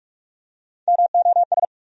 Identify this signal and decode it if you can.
Morse code, heard as a keyed tone, about 46 wpm, 700 Hz. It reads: MOS
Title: USA Call Sign Postfixes - 3 Letters 45wpm